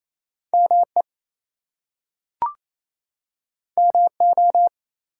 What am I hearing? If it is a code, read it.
Morse code, heard as a keyed tone, about 28 wpm, 700 Hz. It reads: ME MO
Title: US States and Canadian Provinces 28wpm